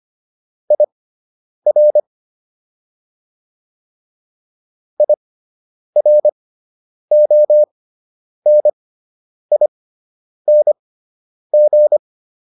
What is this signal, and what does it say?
Morse code, heard as a keyed tone, about 25 wpm, spaced as for 9 wpm, 600 Hz. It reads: IR IRONING